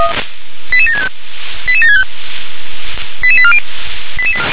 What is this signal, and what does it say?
Five-tone selective call (ZVEI), heard as quick runs of five tones, tones of about 80 ms. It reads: 88765 88765 88840